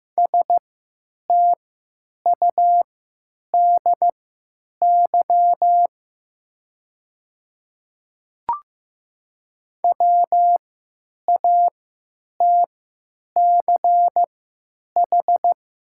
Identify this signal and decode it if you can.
Morse code, heard as a keyed tone, about 15 wpm, spaced as for 9 wpm, 690 Hz. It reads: STUDY WATCH
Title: 3x Character Spacing - Top 300 Words - 15wpm